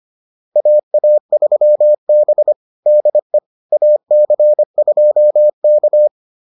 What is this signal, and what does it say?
Morse code, heard as a keyed tone, about 25 wpm, 600 Hz. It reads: AA3B DE AC2K